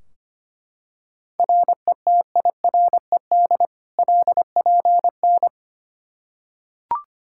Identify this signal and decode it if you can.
Morse code, heard as a keyed tone, about 25 wpm, 700 Hz. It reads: RETIRED LPN
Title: Practice QSO - Exchange 3 - More Info - 25wpm